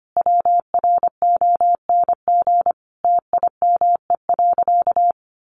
Morse code, heard as a keyed tone, about 25 wpm, 700 Hz. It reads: WRONG TIME.
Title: Sentences from Top 900 Words 25wpm